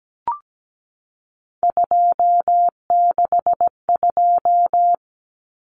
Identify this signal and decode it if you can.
Morse code, heard as a keyed tone, about 17 wpm, 700 Hz. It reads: 262
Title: Numbers - 3-Digit 17wpm